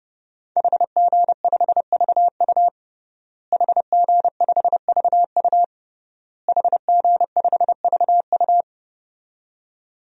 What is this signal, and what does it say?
Morse code, heard as a keyed tone, about 30 wpm, 700 Hz. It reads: HG5VU HG5VU HG5VU
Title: DXCC-100 Call Signs - 5 Characters - Repeated 3x - 30wpm